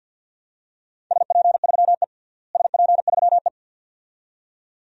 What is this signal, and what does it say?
Morse code, heard as a keyed tone, about 52 wpm, 700 Hz. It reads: SP3E SP3E
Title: DXCC-100 Call Signs - 4 Characters - Repeated 2x - 50wpm